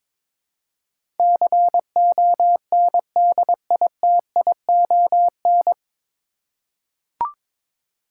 Morse code, heard as a keyed tone, about 22 wpm, 700 Hz. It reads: CONDITION
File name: 9 Letter Words - First 3 Letters Given - 22wpm